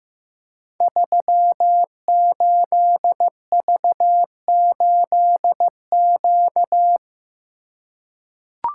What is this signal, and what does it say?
Morse code, heard as a keyed tone, about 15 wpm, 700 Hz. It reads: 38V8Q
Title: Code Groups - 5 Letter:Numbers - 15wpm